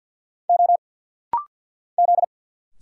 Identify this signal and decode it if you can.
Morse code, heard as a keyed tone, about 47 wpm, 700 Hz. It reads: O Z
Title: Single Letters - Rapid-Fire 50wpm